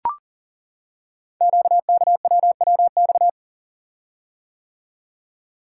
Morse code, heard as a keyed tone, about 40 wpm, 700 Hz. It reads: QKWWX
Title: Code Groups - 5 Letters - 40wpm